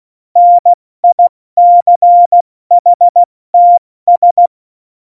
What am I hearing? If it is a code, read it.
Morse code, heard as a keyed tone, about 16 wpm, 700 Hz. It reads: NICHTS
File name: Morsekurs / Witze / Witz 16. April 2019, 70 F80